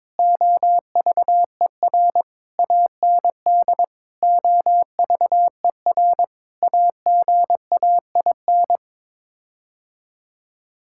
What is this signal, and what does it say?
Morse code, heard as a keyed tone, about 22 wpm, 700 Hz. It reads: OVER AND OVER AGAIN
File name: practice / 22wpm / Sets of 6 Words - 22wpm